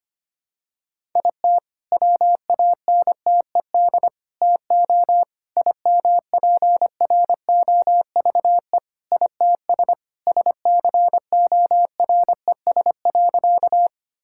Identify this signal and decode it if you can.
Morse code, heard as a keyed tone, about 25 wpm, 700 Hz. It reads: IT WANTED TO IMPROVE ITS SCORES.
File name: practice / 25wpm / All-Age Appropriate Jokes - 25wpm